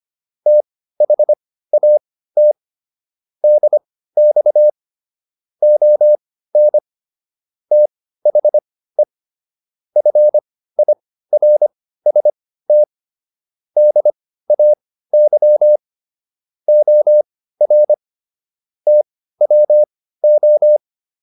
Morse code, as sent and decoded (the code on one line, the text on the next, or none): - .... .- - / -.. -..- / --- -. / - .... / ..-. .. .-. ... - / -.. .- -.-- / --- .-. / - .-- ---
THAT DX ON THE FIRST DAY OR TWO